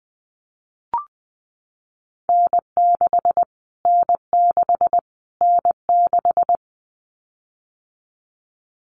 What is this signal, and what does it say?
Morse code, heard as a keyed tone, about 20 wpm, 700 Hz. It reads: N6 N6 N6
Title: Callsign Prefixes for Top 25 DXCC - Repeated 3x - 20wpm